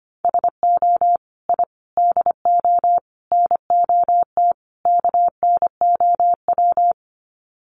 Morse code, heard as a keyed tone, about 25 wpm, 700 Hz. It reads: SO I DO NOT KNOW
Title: Sets of 5 Words Encore 25wpm